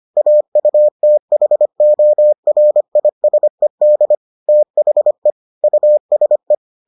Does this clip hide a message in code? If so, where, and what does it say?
Yes: Morse code from 0.2 to 6.5 s, heard as a keyed tone, about 25 wpm, 600 Hz. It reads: AUTHORISED THE USE